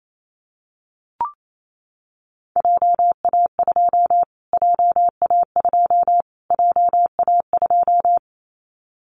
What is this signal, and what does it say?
Morse code, heard as a keyed tone, about 28 wpm, 700 Hz. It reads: JA2 JA2 JA2
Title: Callsign Prefixes for Top 25 DXCC - Repeated 3x - 28wpm